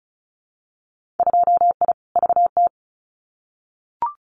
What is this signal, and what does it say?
Morse code, heard as a keyed tone, about 36 wpm, 700 Hz. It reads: 2I VT
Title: Winter Field Day Exchanges 35wpm